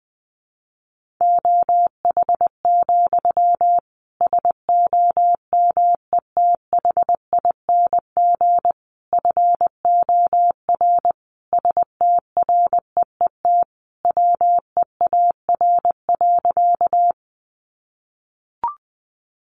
Morse code, heard as a keyed tone, about 20 wpm, 700 Hz. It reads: OH, SOMETHING FOR STREET WEAR.
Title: Sentences from Top 1500 Words 20wpm